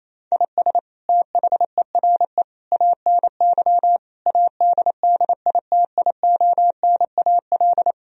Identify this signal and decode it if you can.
Morse code, heard as a keyed tone, about 28 wpm, 700 Hz. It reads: IS THERE ANY ADDITIONAL
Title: Sentences from Top 1000 Words - Review Part 4 - 28wpm